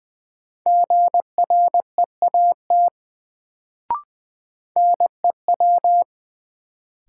Morse code, heard as a keyed tone, about 20 wpm, 700 Hz. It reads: GREAT NEW
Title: Sets of 2 Words - New Word Drill - 20wpm